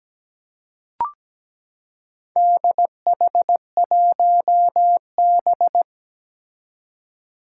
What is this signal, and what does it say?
Morse code, heard as a keyed tone, about 17 wpm, 700 Hz. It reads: DH1B
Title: Four Challenging Letter-Number Combinations 17wpm